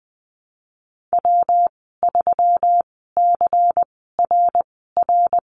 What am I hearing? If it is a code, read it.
Morse code, heard as a keyed tone, about 20 wpm, spaced as for 14 wpm, 700 Hz. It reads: W3CRR
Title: 122 Lesson 20g - Number 3 - Call Signs 20wpm